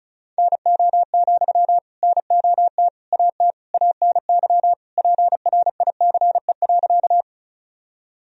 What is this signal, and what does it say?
Morse code, heard as a keyed tone, about 36 wpm, 700 Hz. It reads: NO, NOT AT ANY PRICE.
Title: Sentences from Top 1000 Words - Review Part 4 - 35wpm